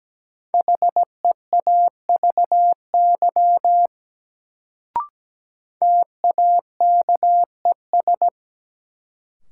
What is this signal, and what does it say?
Morse code, heard as a keyed tone, about 17 wpm, 700 Hz. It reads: HEAVY TAKES
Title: Sentences from Top 1000 Words - Review Part 6 - New Word Drill - 17wpm